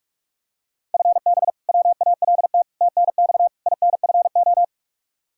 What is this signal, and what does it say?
Morse code, heard as a keyed tone, about 42 wpm, 700 Hz. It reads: FB WALT TNX INFO
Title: Practice QSO - Exchange 3 - More Info - 45wpm